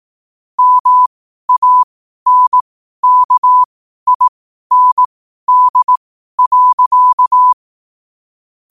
Morse code, heard as a keyed tone, about 18 wpm, spaced as for 12 wpm, 1000 Hz. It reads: MANKIND.
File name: qotdfiles / 12wpm